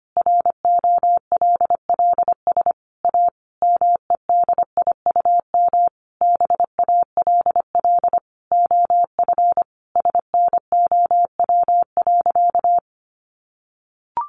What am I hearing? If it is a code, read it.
Morse code, heard as a keyed tone, about 25 wpm, 700 Hz. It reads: ROLLS A MEDIUM BALL OF SNOW.
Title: Easy Phrases 25wpm